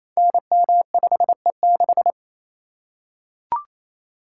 Morse code, heard as a keyed tone, about 27 wpm, 700 Hz. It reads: NM5E6
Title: Code Groups - 5 Letter:Numbers - 28wpm